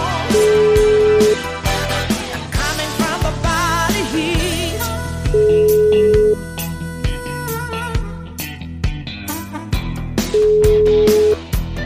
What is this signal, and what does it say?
Ringback tone (Europe, 425 Hz), with noise.